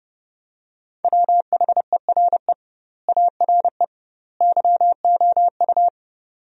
Morse code, heard as a keyed tone, about 30 wpm, 700 Hz. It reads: WHERE ARE YOU
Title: 2x Word-Spacing - Sentences from Top 400 Words - 30wpm